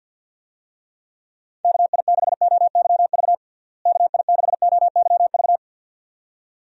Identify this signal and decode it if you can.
Morse code, heard as a keyed tone, about 52 wpm, 700 Hz. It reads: KI6OYV KI6OYV
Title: USA Call Signs - 6 Characters - Repeated 2x - 50wpm